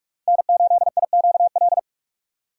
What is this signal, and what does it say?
Morse code, heard as a keyed tone, about 45 wpm, 700 Hz. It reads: N8IQL
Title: USA Call Signs - 2x Character Spacing - 45wpm